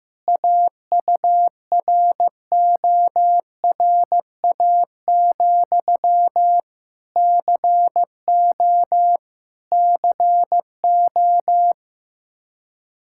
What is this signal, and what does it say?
Morse code, heard as a keyed tone, about 15 wpm, 700 Hz. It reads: AURORA, CO CO
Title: Practice QSO - Exchange 1 - Beginning - 15wpm